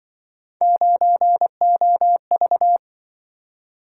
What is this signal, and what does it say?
Morse code, heard as a keyed tone, about 25 wpm, 700 Hz. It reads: 9OV
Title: Speed-Racing - Three Letter-Number Combinations - 20wpm